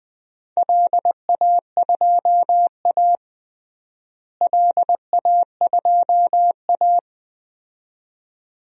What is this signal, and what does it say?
Morse code, heard as a keyed tone, about 20 wpm, 700 Hz. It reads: LA2A LA2A
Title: DXCC-100 Call Signs - 4 Characters - Repeated 2x - 20wpm